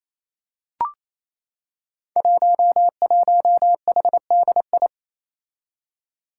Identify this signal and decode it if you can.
Morse code, heard as a keyed tone, about 27 wpm, 700 Hz. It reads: 11HDI
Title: Five Challenging Letter-Number Combinations 28wpm